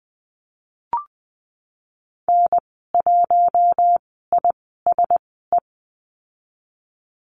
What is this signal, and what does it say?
Morse code, heard as a keyed tone, about 20 wpm, spaced as for 15 wpm, 700 Hz. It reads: N1ISE
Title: 062 Lesson 11g - Letter h - Call Signs 20wpm